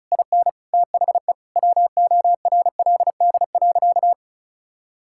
Morse code, heard as a keyed tone, about 34 wpm, 700 Hz. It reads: IN THE WORLD.
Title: Sentences from Top 1600 Words 35wpm